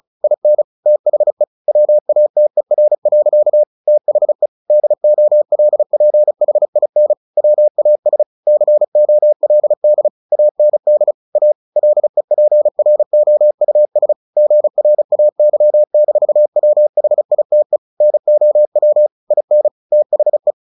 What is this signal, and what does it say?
Morse code, heard as a keyed tone, about 34 wpm, 600 Hz. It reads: IN THE WATER. THE DOLPHIN WAS COLD AND A LEPROUS GRAY-WHITE NOW IN THE
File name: the old man and the sea 2 35wpm